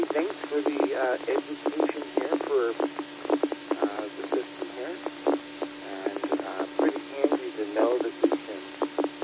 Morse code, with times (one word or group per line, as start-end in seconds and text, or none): none